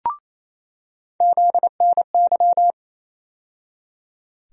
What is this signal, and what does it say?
Morse code, heard as a keyed tone, about 28 wpm, 700 Hz. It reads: ZNY
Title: USA Call Sign Postfixes - 3 Letters 28wpm